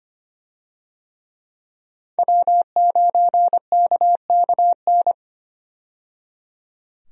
Morse code, heard as a keyed tone, about 25 wpm, 700 Hz. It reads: W9KKN
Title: USA Call Signs - 5 Characters - 25wpm